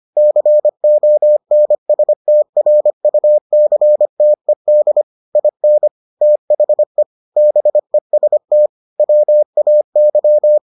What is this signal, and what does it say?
Morse code, heard as a keyed tone, about 25 wpm, 600 Hz. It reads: CONSTRUCTED IN THE BEST WAY